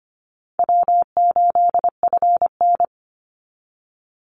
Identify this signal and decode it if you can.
Morse code, heard as a keyed tone, about 25 wpm, 700 Hz. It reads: W8FN
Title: Easy Call Signs - 25wpm